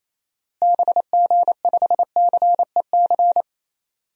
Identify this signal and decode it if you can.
Morse code, heard as a keyed tone, about 28 wpm, 700 Hz. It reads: BG5CEC
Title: DXCC-100 Call Signs - 6 Characters - Repeated 3x - 28wpm